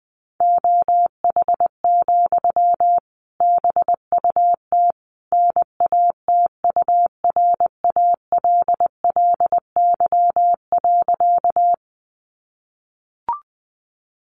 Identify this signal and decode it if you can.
Morse code, heard as a keyed tone, about 20 wpm, 700 Hz. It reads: OH, BUT NATURALLY.
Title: Sentences from Top 1300 Words 20wpm